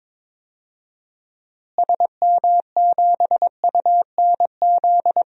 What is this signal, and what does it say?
Morse code, heard as a keyed tone, about 22 wpm, 700 Hz. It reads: SM7UNZ
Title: DXCC-100 Call Signs - 6 Characters - Repeated 2x - 22wpm